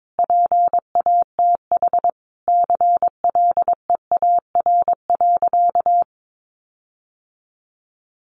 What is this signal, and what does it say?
Morse code, heard as a keyed tone, about 22 wpm, 700 Hz. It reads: PATH CLEAR.
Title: Sentences from Top 1400 Words 22wpm